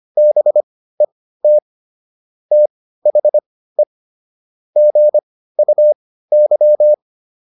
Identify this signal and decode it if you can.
Morse code, heard as a keyed tone, about 25 wpm, spaced as for 15 wpm, 600 Hz. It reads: BET THE GUY